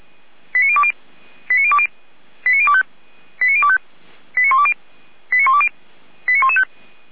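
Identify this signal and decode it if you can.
Five-tone selective call (ZVEI), heard as quick runs of five tones, tones of about 80 ms. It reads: 89020 89020 89025 89025 89120 89120 89115
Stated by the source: ZVEI five-tone call 89270, received by SDR.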